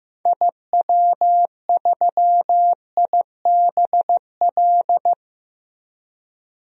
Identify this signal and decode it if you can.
Morse code, heard as a keyed tone, about 15 wpm, 700 Hz. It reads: IW3IBL
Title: DXCC-100 Call Signs - 6 Characters - Repeated 3x - 15wpm